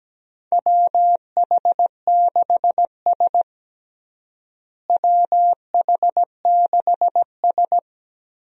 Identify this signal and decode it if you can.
Morse code, heard as a keyed tone, about 17 wpm, 700 Hz. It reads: WH6S WH6S
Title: USA Call Signs - 4 Characters - Repeated 2x - 17wpm